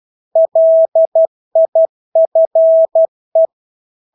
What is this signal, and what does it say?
Morse code, heard as a keyed tone, about 12 wpm, 650 Hz. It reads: LIFE
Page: Engelske ord 12 wpm